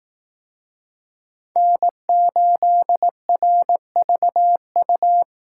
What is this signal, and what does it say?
Morse code, heard as a keyed tone, about 18 wpm, 700 Hz. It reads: N8RVU
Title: Speed Racing - USA 1x3 Call Signs 15wpm